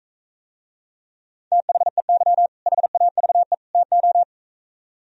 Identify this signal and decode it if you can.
Morse code, heard as a keyed tone, about 45 wpm, 700 Hz. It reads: THEY HAVE TO